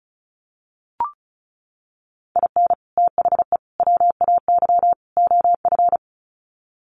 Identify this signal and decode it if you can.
Morse code, heard as a keyed tone, about 34 wpm, 700 Hz. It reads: IN THE WAY OF
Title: Sets of 4 Words 35wpm